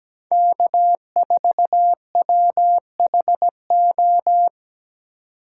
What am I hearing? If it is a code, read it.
Morse code, heard as a keyed tone, about 17 wpm, 700 Hz. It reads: K4WHO
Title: USA Call Signs - 5 Characters - 2x Character Spacing - 17wpm